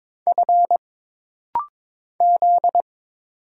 Morse code, heard as a keyed tone, about 22 wpm, 700 Hz. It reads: F Z
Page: This rapid-fire timing is geared to push your ICR (Instant Character Recognition) skills.